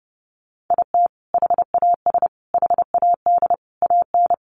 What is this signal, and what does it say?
Morse code, heard as a keyed tone, about 30 wpm, 700 Hz. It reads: IT HAS HAD AN
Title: Sentences from Top 1100 Words 30wpm